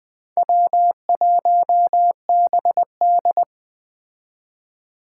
Morse code, heard as a keyed tone, about 19 wpm, 700 Hz. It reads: W1BD